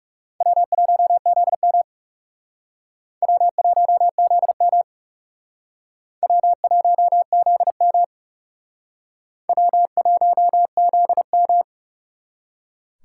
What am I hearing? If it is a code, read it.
Morse code, heard as a keyed tone, about 36 wpm, 700 Hz. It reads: W1ZM W1ZM W1ZM W1ZM
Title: Speed Racing - USA 1x2 Call Signs 30wpm